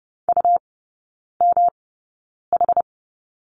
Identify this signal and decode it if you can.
Morse code, heard as a keyed tone, about 30 wpm, spaced as for 9 wpm, 700 Hz. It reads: UMH
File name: Lesson 18e - Letter m - Three Characters 30wpm